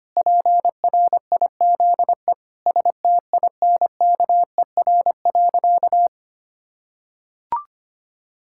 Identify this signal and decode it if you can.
Morse code, heard as a keyed tone, about 25 wpm, 700 Hz. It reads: PRIZE STINKER.
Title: Pangram Challenge v2 - 25wpm